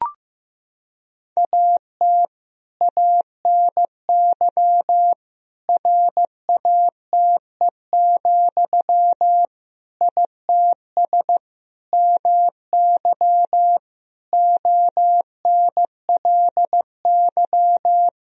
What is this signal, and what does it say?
Morse code, heard as a keyed tone, about 15 wpm, 700 Hz. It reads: AT ANY RATE, ITS MY ONLY
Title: Sentences from Top 1000 Words - Review Part 4 - 15wpm